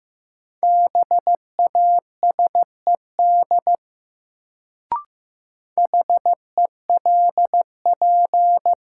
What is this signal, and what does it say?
Morse code, heard as a keyed tone, about 15 wpm, 700 Hz. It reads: BASED HELP
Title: Sets of 2 Words - New Word Drill - 15wpm